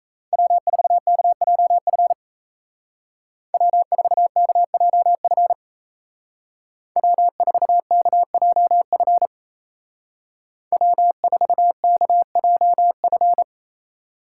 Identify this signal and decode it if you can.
Morse code, heard as a keyed tone, about 37 wpm, 700 Hz. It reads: W4KJF W4KJF W4KJF W4KJF